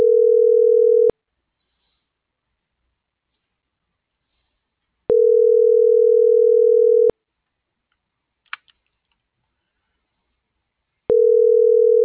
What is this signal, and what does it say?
Ringback tone (North America), 440 + 480 Hz.